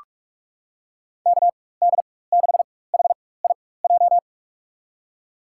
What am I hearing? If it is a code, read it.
Morse code, heard as a keyed tone, about 46 wpm, spaced as for 21 wpm, 700 Hz. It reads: KD6HIJ
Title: USA Call Signs - 6 Characters - 4x Character Spacing - 45wpm